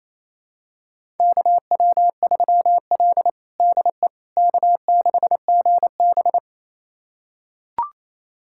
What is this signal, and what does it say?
Morse code, heard as a keyed tone, about 28 wpm, 690 Hz. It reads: KW3L DE K6GB
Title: USA Callsign DE Callsign - 4 Characters - 28wpm